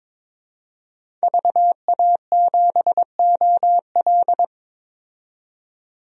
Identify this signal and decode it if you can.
Morse code, heard as a keyed tone, about 22 wpm, 700 Hz. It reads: VA7OL
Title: DXCC-100 Call Signs - 5 Characters - 22wpm